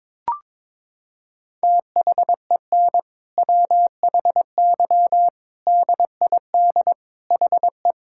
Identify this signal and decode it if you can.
Morse code, heard as a keyed tone, about 22 wpm, 700 Hz. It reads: THEN WHY DID HE
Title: Sentences from Top 1000 Words - Review Part 5 - 22wpm